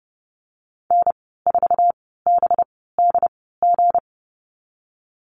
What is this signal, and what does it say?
Morse code, heard as a keyed tone, about 30 wpm, spaced as for 17 wpm, 700 Hz. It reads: N4BDG
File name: Lesson 37g - Letter x - Call Signs 30wpm